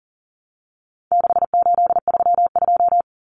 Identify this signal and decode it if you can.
Morse code, heard as a keyed tone, about 40 wpm, 700 Hz. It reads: 6832
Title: Numbers - 4-Digit 40wpm